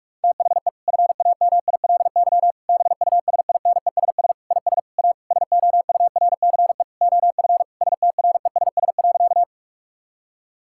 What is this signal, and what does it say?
Morse code, heard as a keyed tone, about 41 wpm, 700 Hz. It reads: THE FAMILY BUSINESS IS A SOURCE OF STRESS.
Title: Sentences from Top 1000 Words - Review Part 6 - 45wpm